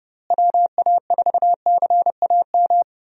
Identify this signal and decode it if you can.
Morse code, heard as a keyed tone, about 30 wpm, 700 Hz. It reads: WA4CAM
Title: USA Call Signs - 6 Characters - Repeated 2x - 30wpm